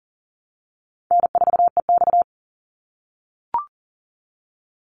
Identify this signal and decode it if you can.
Morse code, heard as a keyed tone, about 40 wpm, 700 Hz. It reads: N4EX
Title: NA SOTA Chaser Callsigns 40wpm